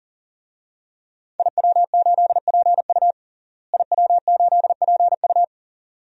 Morse code, heard as a keyed tone, about 40 wpm, 700 Hz. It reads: IW8PU IW8PU
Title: DXCC-100 Call Signs - 5 Characters - Repeated 2x - 40wpm